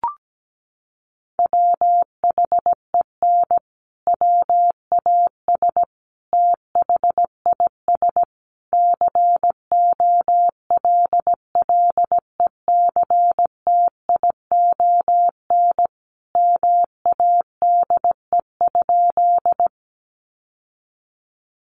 Morse code, heard as a keyed tone, about 17 wpm, 700 Hz. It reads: WHEN WAS THIS COLLECTION MADE?
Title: Sentences from Top 1200 Words 17wpm